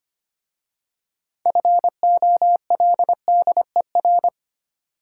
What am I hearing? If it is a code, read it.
Morse code, heard as a keyed tone, about 25 wpm, 700 Hz. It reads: FOLDER
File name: practice / 25wpm / Difficult Words with Hints - 25wpm